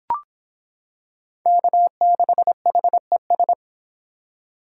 Morse code, heard as a keyed tone, about 25 wpm, 700 Hz. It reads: K6HES